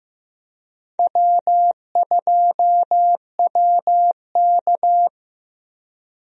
Morse code, heard as a keyed tone, about 15 wpm, 700 Hz. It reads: W2WK
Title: USA Call Signs - 4 Characters - Repeated 3x - 15wpm